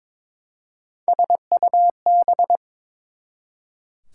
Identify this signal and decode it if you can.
Morse code, heard as a keyed tone, about 22 wpm, 700 Hz. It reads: SUB
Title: Prefixes and Suffixes 22wpm